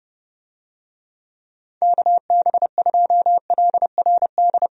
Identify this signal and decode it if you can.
Morse code, heard as a keyed tone, about 30 wpm, 700 Hz. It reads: KB2LRD
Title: USA Call Signs - 6 Characters - Repeated 3x - 30wpm